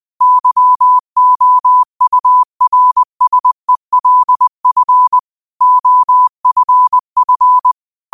Morse code, heard as a keyed tone, about 20 wpm, 1000 Hz. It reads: YOURSELF OFF
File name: qotdfiles / 20wpm